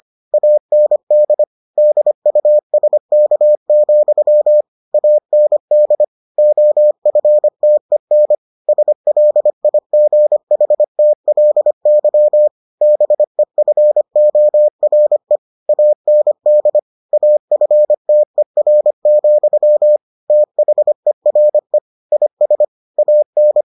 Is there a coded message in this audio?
Yes: Morse code from 0.3 to 23.6 s, heard as a keyed tone, about 25 wpm, 600 Hz. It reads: AND DUSK, AND OFTEN SLIGHTLY BEFORE AND AFTER, THERE IS AN